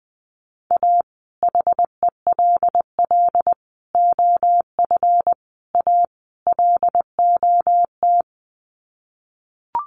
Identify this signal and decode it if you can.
Morse code, heard as a keyed tone, about 20 wpm, 700 Hz. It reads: A HELL OF A LOT
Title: Sets of 5 Words Encore 20wpm